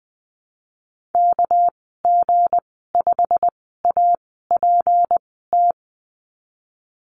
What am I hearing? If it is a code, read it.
Morse code, heard as a keyed tone, about 19 wpm, spaced as for 14 wpm, 700 Hz. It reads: KG5APT
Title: 213 Lesson 34g - Number 8 - Call Signs 20wpm